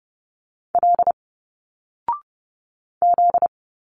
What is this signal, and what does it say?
Morse code, heard as a keyed tone, about 30 wpm, 700 Hz. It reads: L Z
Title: Single Letters 30wpm